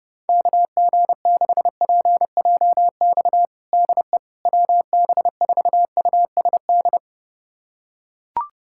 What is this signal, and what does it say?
Morse code, heard as a keyed tone, about 30 wpm, 700 Hz. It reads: KG6PJX DE WB4USD
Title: USA Callsign DE Callsign - 6 Characters - 30wpm